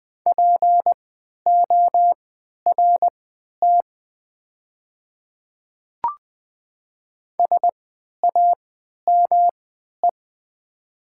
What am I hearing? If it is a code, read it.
Morse code, heard as a keyed tone, about 20 wpm, spaced as for 11 wpm, 690 Hz. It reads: PORT SAME
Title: 3x Character Spacing - Top 200 Words - 20wpm